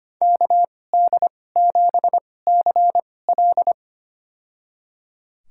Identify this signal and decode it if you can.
Morse code, heard as a keyed tone, about 25 wpm, spaced as for 18 wpm, 700 Hz. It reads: KD7CL
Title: USA Call Signs - 5 Characters - 2x Character Spacing - 25wpm